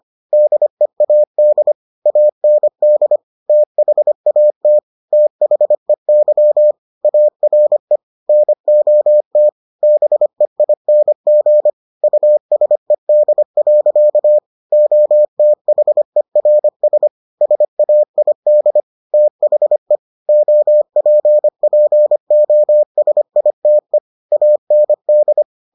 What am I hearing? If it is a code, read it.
Morse code, heard as a keyed tone, about 25 wpm, 600 Hz. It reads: DEAD AND THAT THEY ARE NOT BEING USED. OTHERS SAID THE OPPOSITE AND